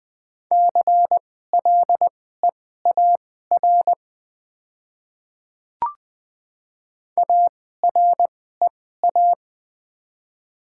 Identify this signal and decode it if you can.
Morse code, heard as a keyed tone, about 20 wpm, spaced as for 14 wpm, 690 Hz. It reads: CLEAR AREA
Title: 2x Character Spacing - Top 500 Words - 20wpm